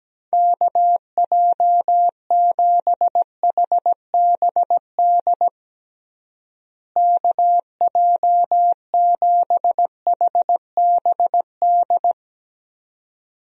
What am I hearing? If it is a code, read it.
Morse code, heard as a keyed tone, about 17 wpm, 700 Hz. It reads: KJ7HBD KJ7HBD
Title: USA Call Signs - 6 Characters - Repeated 2x - 17wpm